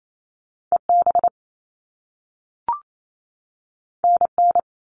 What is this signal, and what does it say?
Morse code, heard as a keyed tone, about 28 wpm, 700 Hz. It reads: EB NN
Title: All Cut Numbers - 2-Digit 28wpm